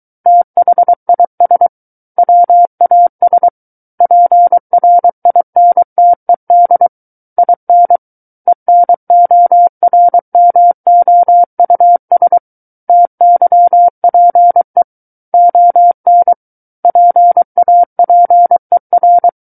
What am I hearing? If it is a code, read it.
Morse code, heard as a keyed tone, about 23 wpm, 700 Hz. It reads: THIS WAS PRINTED IN ENORMOUS TYPE ON PAPER
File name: Book One / 14-Chapter 14 - 23 WPM